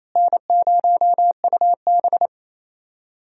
Morse code, heard as a keyed tone, about 27 wpm, 700 Hz. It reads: N0UB
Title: USA Call Signs - 4 Characters - Repeated 2x - 28wpm